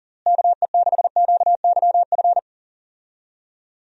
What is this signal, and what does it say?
Morse code, heard as a keyed tone, about 37 wpm, 700 Hz. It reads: KE6QYF
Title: USA Call Signs - 6 Characters - Repeated 3x - 40wpm